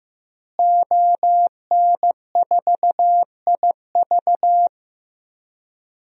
Morse code, heard as a keyed tone, about 15 wpm, 700 Hz. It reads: ON4IV
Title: DXCC-100 Call Signs - 5 Characters - Repeated 3x - 15wpm